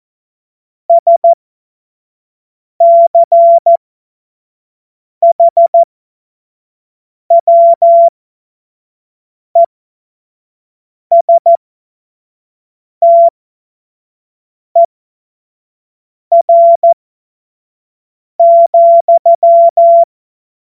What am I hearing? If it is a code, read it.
Morse code, heard as a keyed tone, about 14 wpm, spaced as for 5 wpm, 680 Hz. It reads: SCHWESTER,
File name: Morsekurs / Witze / Witz 16 April 2024 25 F70